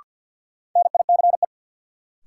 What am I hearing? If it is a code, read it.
Morse code, heard as a keyed tone, about 53 wpm, 700 Hz. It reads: NICE
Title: Sentences from Top 1100 Words - New Word Drill - 50wpm